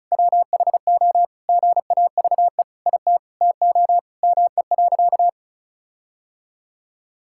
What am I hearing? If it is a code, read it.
Morse code, heard as a keyed tone, about 36 wpm, 700 Hz. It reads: WHO GAVE IT TO ME.
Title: Sentences from Top 1600 Words 35wpm